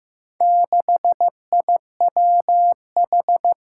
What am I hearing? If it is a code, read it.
Morse code, heard as a keyed tone, about 15 wpm, 700 Hz. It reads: 6IWH
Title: Four Challenging Letter-Number Combinations 15wpm